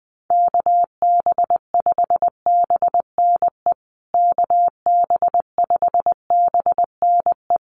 Morse code, heard as a keyed tone, about 20 wpm, 700 Hz. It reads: KB5BNE KB5BNE
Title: USA Call Signs Repeated 3x 20wpm